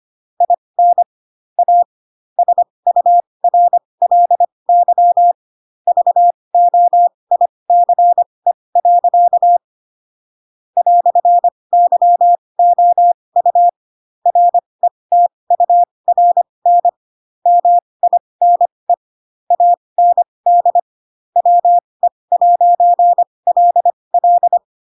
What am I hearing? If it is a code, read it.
Morse code, heard as a keyed tone, about 25 wpm, spaced as for 20 wpm, 700 Hz. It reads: IN A SURLY VOICE. "YOU RETURN MINE AND WE'LL
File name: Pinocchio / Pinocchio ch2 15 25 ik1hge